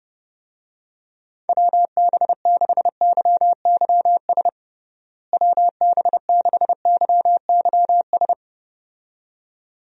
Morse code, heard as a keyed tone, about 30 wpm, 700 Hz. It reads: WB6YYS WB6YYS
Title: USA Call Signs - 6 Characters - Repeated 2x - 30wpm